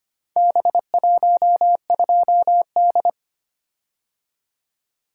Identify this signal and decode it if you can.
Morse code, heard as a keyed tone, about 25 wpm, 700 Hz. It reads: B12D
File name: practice / 25wpm / Four Letter-Number Combinations - 25wpm